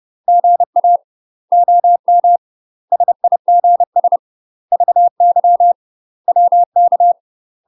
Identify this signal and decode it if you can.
Morse code, heard as a keyed tone, about 30 wpm, 700 Hz. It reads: GA OM SIGS VY WK
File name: UK MASTER QSOs TRAINING / 30WPM / UK MASTER QSOs 0014